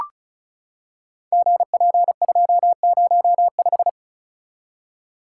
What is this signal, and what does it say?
Morse code, heard as a keyed tone, about 36 wpm, 700 Hz. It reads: GP205
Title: Code Groups - 5 Letter:Numbers - 35wpm